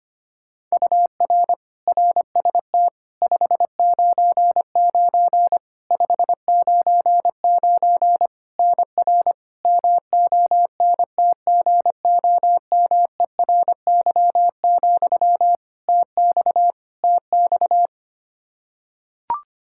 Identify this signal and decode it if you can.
Morse code, heard as a keyed tone, about 25 wpm, 700 Hz. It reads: UR RST 599 599 NR MONTGOMERY, TX TX
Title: Practice QSO - Exchange 1 - Beginning - 25wpm